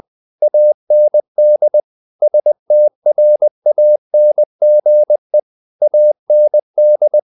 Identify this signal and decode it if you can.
Morse code, heard as a keyed tone, about 20 wpm, 600 Hz. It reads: AND STRANGE AND
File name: the old man and the sea / the old man and the sea part 1 20wpm